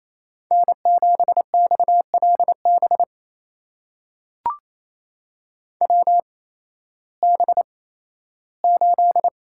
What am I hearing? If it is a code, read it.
Morse code, heard as a keyed tone, about 28 wpm, spaced as for 15 wpm, 700 Hz. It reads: N7XLB WB8
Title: USA Call Signs - 8x Character Spacing - 28wpm